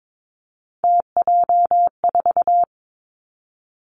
Morse code, heard as a keyed tone, about 22 wpm, 700 Hz. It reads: TJ4
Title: Three Challenging Letter-Number Combinations 22wpm